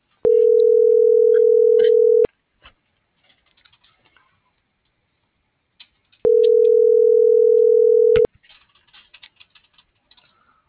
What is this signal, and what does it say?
Ringback tone (North America), 440 + 480 Hz.
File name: CENTRAL MULTIMODAL / out cecisandoval